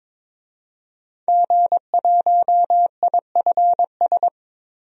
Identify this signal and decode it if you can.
Morse code, heard as a keyed tone, about 22 wpm, 700 Hz. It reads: G1IFS
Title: DXCC-100 Call Signs - 5 Characters - Repeated 3x - 22wpm